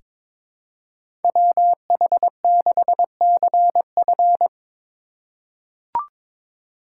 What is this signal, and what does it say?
Morse code, heard as a keyed tone, about 22 wpm, 690 Hz. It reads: WH6CF
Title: USA Call Signs - 5 Characters - 2x Character Spacing - 22wpm